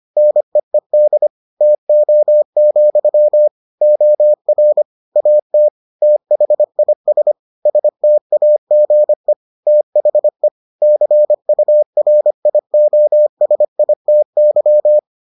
Morse code, heard as a keyed tone, about 25 wpm, 600 Hz. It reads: NEED TO, OR AT THIS STAGE THE CURIOSITY